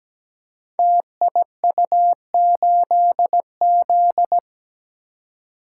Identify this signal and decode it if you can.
Morse code, heard as a keyed tone, about 17 wpm, 700 Hz. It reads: TIU8Z
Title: Code Groups - 5 Letter:Numbers - 17wpm